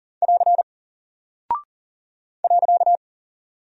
Morse code, heard as a keyed tone, about 40 wpm, 700 Hz. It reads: <AR> .
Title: Single Character (Letter:Number:Punctuation:Prosign) 40wpm